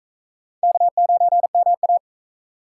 Morse code, heard as a keyed tone, about 40 wpm, 700 Hz. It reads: K9MA